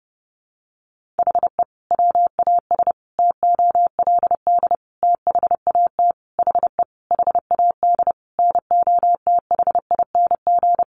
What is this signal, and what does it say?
Morse code, heard as a keyed tone, about 30 wpm, 700 Hz. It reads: HE WAS TOLD THAT HE HAD NOTHING
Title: Sentences from Top 1500 Words 30wpm